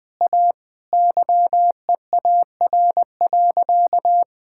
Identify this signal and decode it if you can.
Morse code, heard as a keyed tone, about 20 wpm, 700 Hz. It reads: A YEAR.
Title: Sentences from Top 1000 Words - Review Part 5 - 20wpm